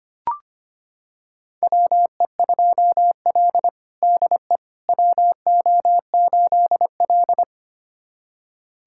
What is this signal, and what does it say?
Morse code, heard as a keyed tone, about 25 wpm, 700 Hz. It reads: WE2L DE WO8L
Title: USA Callsign DE Callsign - 4 Characters - 25wpm